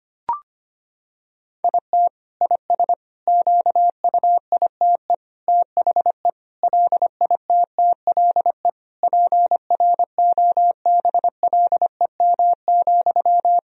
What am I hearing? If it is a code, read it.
Morse code, heard as a keyed tone, about 25 wpm, 700 Hz. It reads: IT IS QUITE THE LITTLE PROBLEM,
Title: Sentences from Top 1000 Words - Review Part 2 - 25wpm